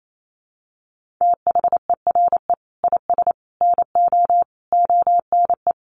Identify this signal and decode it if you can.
Morse code, heard as a keyed tone, about 28 wpm, 700 Hz. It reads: THERE IS NO ONE
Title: Sentences from Top 1100 Words 28wpm